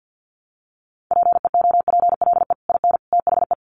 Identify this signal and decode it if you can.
Morse code, heard as a keyed tone, about 50 wpm, 700 Hz. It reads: PEOPLE IN THE
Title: Sets of 3 Words 50wpm